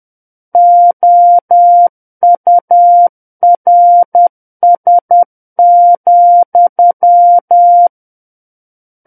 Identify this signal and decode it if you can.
Morse code, heard as a keyed tone, about 10 wpm, 700 Hz. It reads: OURS,
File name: Book One / 01-Chapter 1 - 10 WPM